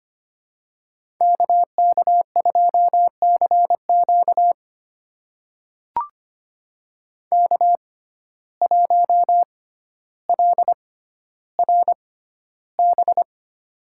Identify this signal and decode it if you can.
Morse code, heard as a keyed tone, about 25 wpm, spaced as for 13 wpm, 700 Hz. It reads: KK2CQ K1LRB
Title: USA Call Signs - 5 Characters - 6x Character Spacing - 25wpm